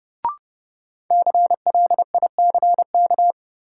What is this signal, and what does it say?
Morse code, heard as a keyed tone, about 32 wpm, 700 Hz. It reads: CLICK
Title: Sentences from Top 1100 Words - New Word Drill - 30wpm